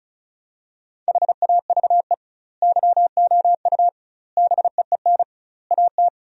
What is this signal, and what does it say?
Morse code, heard as a keyed tone, about 36 wpm, 700 Hz. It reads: HAVE YOU BEEN AT
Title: 2x Word-Spacing - Sentences from Top 300 Words - 35wpm